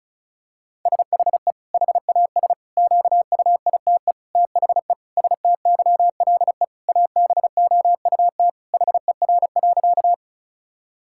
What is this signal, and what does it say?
Morse code, heard as a keyed tone, about 36 wpm, 700 Hz. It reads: SHE HAS QUITE THE STYLE ABOUT HER.
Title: Sentences from Top 1000 Words - Review Part 5 - 35wpm